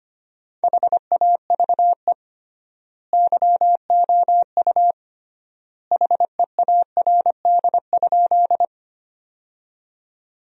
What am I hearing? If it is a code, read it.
Morse code, heard as a keyed tone, about 25 wpm, 700 Hz. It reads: HAVE YOU HEARD?
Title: 3x Word-Spacing - Sentences from Top 400 Words - 25wpm